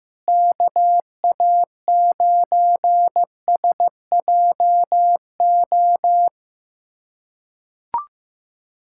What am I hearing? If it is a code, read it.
Morse code, heard as a keyed tone, about 15 wpm, 700 Hz. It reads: KA9SJO
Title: USA Callsign DE Callsign - 6 Characters - 15wpm